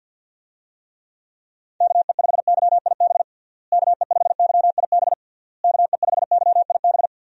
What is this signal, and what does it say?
Morse code, heard as a keyed tone, about 52 wpm, 700 Hz. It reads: KE5YIB KE5YIB KE5YIB
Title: USA Call Signs - 6 Characters - Repeated 3x - 50wpm